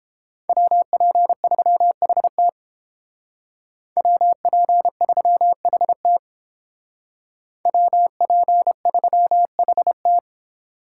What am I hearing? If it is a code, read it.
Morse code, heard as a keyed tone, about 30 wpm, 700 Hz. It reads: WP3HT WP3HT WP3HT